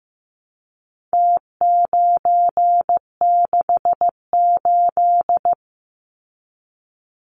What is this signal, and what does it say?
Morse code, heard as a keyed tone, about 15 wpm, 700 Hz. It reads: T968
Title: Numbers - Contest Style - 4-Digit 15wpm